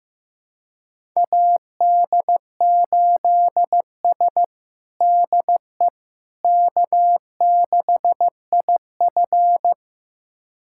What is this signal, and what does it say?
Morse code, heard as a keyed tone, about 15 wpm, 700 Hz. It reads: AD8S DE K6IF
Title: USA Callsign DE Callsign - 4 Characters - 15wpm